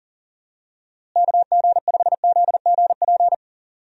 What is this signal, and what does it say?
Morse code, heard as a keyed tone, about 40 wpm, 700 Hz. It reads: KG5ZGP
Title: USA Call Signs - 6 Characters - Repeated 3x - 40wpm